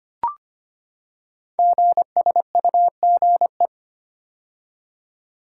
Morse code, heard as a keyed tone, about 25 wpm, 700 Hz. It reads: GSUGE
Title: Code Groups - 5 Letters - 25wpm